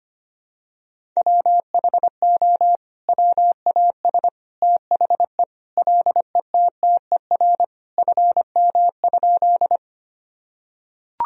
Morse code, heard as a keyed tone, about 25 wpm, 700 Hz. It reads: WHO WAS THE LETTER FM?
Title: Sentences from Top 300 Words and Abbreviations 25wpm